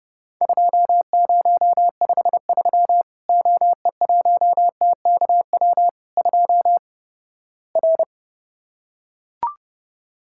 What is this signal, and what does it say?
Morse code, heard as a keyed tone, about 30 wpm, 700 Hz. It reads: 2053 OE1TKW 2 R
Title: Worked All Europe DX Contest - QTC Exchange - 30wpm